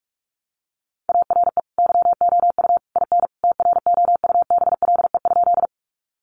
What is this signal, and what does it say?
Morse code, heard as a keyed tone, about 44 wpm, 700 Hz. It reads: ARE YOU IN TROUBLE?
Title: Sentences from Top 1200 Words 45wpm